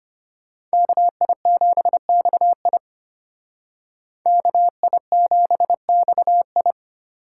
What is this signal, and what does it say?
Morse code, heard as a keyed tone, about 31 wpm, 700 Hz. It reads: KI7XI KI7XI
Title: Speed Racing - USA 2x2 Call Signs 25wpm